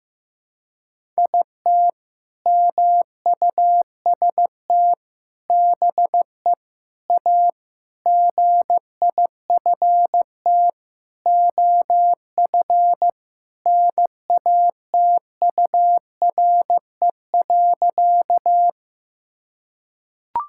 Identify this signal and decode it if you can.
Morse code, heard as a keyed tone, about 15 wpm, 700 Hz. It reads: IT MUST BE A GIFT OF NATURE.
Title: Sentences from Top 1000 Words - Review Part 6 - 15wpm